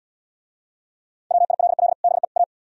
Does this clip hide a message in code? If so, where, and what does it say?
Yes: Morse code from 1.3 to 2.4 s, heard as a keyed tone, about 76 wpm, 700 Hz. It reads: WILL BE A